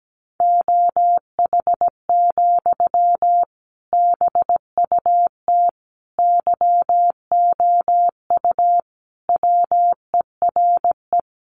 Morse code, heard as a keyed tone, about 17 wpm, 700 Hz. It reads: OH, BUT YOU WERE
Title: Sentences from Top 500 Words 17wpm